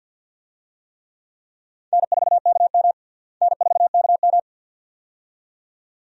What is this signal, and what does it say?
Morse code, heard as a keyed tone, about 46 wpm, 700 Hz. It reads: N4KM N4KM
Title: USA Call Signs - 4 Characters - Repeated 2x - 50wpm